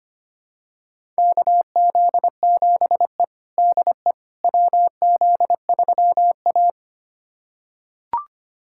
Morse code, heard as a keyed tone, about 25 wpm, 700 Hz. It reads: KZ7E DE WZ3A
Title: USA Callsign DE Callsign - 4 Characters - 25wpm